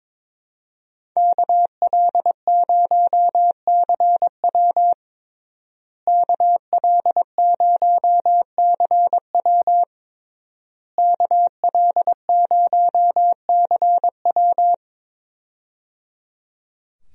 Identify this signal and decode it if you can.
Morse code, heard as a keyed tone, about 22 wpm, 700 Hz. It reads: KL0CW KL0CW KL0CW
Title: USA Call Signs - 5 Characters - Repeated 3x - 22wpm